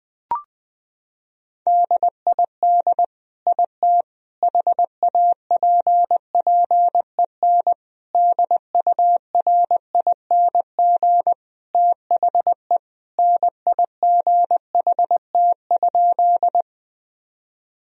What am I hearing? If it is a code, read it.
Morse code, heard as a keyed tone, about 20 wpm, 700 Hz. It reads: DID IT HAPPEN DURING THE NIGHT?
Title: Sentences from Top 1000 Words - Review Part 2 - 20wpm